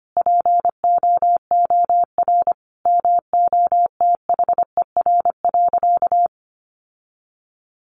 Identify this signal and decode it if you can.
Morse code, heard as a keyed tone, about 25 wpm, 700 Hz. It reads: POOR MOTHER.
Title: Sentences from Top 1000 Words 25wpm